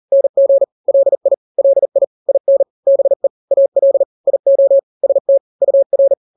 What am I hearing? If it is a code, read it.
Morse code, heard as a keyed tone, about 40 wpm, 550 Hz. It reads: NG LI LI IN BE AL IO ST UR
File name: n-grams / 2-grams 40 35 WPM